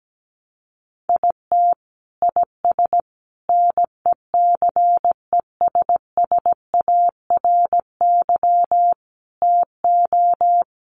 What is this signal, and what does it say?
Morse code, heard as a keyed tone, about 17 wpm, 700 Hz. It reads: IT IS NECESSARY TO
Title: Sentences from Top 800 Words 17wpm